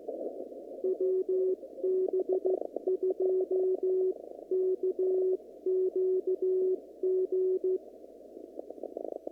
Morse code, heard as a keyed tone, about 15 wpm, 370 Hz. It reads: WB2KQG
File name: CW / kt5j wpx cw-op-k5pi